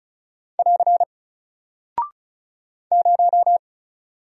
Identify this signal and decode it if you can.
Morse code, heard as a keyed tone, about 34 wpm, 700 Hz. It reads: <AR> 0
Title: Single Character (Letter:Number:Punctuation:Prosign) Alt 35wpm